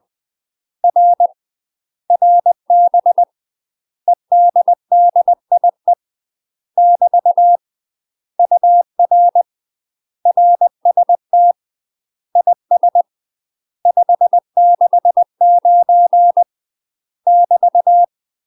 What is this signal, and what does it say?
Morse code, heard as a keyed tone, about 20 wpm, 700 Hz. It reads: R RB EDDIE <BT> UR RST IS 569 <BT>
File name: MASTER QSOs TRAINING/20WPM